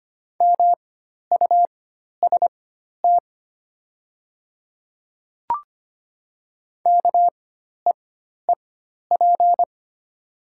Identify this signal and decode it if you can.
Morse code, heard as a keyed tone, about 25 wpm, spaced as for 12 wpm, 700 Hz. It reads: MUST KEEP
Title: 4x Character Spacing - Top 300 Words - 25wpm